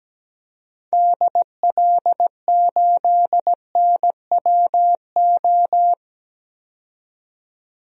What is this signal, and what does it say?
Morse code, heard as a keyed tone, about 17 wpm, 700 Hz. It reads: DL8NWO
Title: DXCC-100 Call Signs - 6 Characters - 17wpm